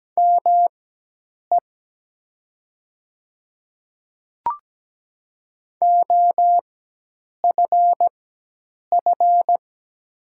Morse code, heard as a keyed tone, about 17 wpm, spaced as for 8 wpm, 700 Hz. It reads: ME OFF
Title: 4x Character Spacing - Top 300 Words - 17wpm